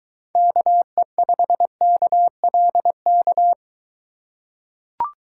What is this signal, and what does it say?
Morse code, heard as a keyed tone, about 23 wpm, 700 Hz. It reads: KE5KLK